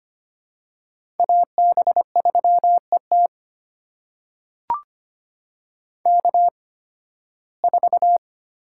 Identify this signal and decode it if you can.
Morse code, heard as a keyed tone, about 25 wpm, spaced as for 16 wpm, 700 Hz. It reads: AB3ET K4
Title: USA Call Signs - 5 Characters - 8x Character Spacing - 25wpm